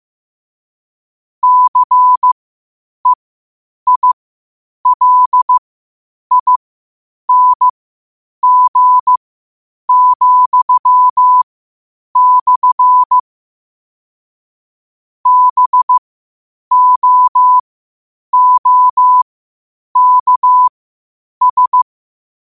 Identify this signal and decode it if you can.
Morse code, heard as a keyed tone, about 15 wpm, spaced as for 8 wpm, 1000 Hz. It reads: CEILING,/ BOOKS
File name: qotdfiles / 8wpm